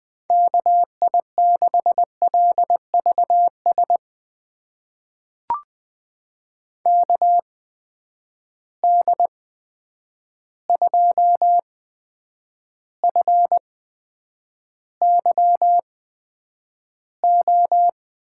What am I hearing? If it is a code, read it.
Morse code, heard as a keyed tone, about 20 wpm, spaced as for 9 wpm, 700 Hz. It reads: KI6LVS KD2FYO
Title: USA Call Signs - 6 Characters - 8x Character Spacing - 20wpm